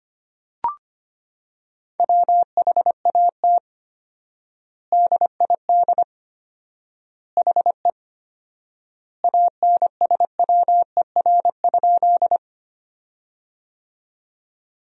Morse code, heard as a keyed tone, about 25 wpm, 700 Hz. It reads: WHAT DID HE ANSWER?
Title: 4x Word-Spacing - Sentences from Top 300 Words - 25wpm